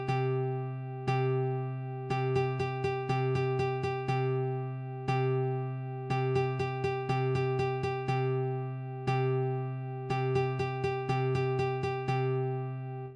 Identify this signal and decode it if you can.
There is no code.